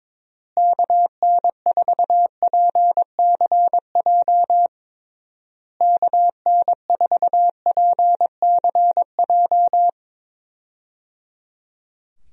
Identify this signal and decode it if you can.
Morse code, heard as a keyed tone, about 22 wpm, 700 Hz. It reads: KN4PCJ KN4PCJ
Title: USA Call Signs - 6 Characters - Repeated 3x - 22wpm